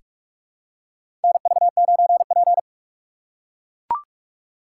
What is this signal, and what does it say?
Morse code, heard as a keyed tone, about 43 wpm, 700 Hz. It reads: NV9P
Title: USA Call Signs - 4 Characters - 3x Character Spacing - 45wpm